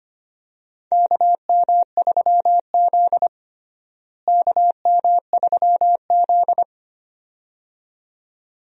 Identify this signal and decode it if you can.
Morse code, heard as a keyed tone, about 25 wpm, 700 Hz. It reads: KM3Z KM3Z
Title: USA Call Signs - 4 Characters - Repeated 2x - 25wpm